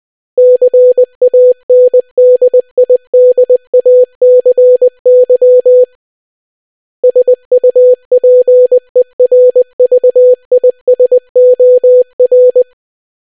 Morse code, heard as a keyed tone, about 20 wpm, 500 Hz. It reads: CANDIDACY SUPERVISOR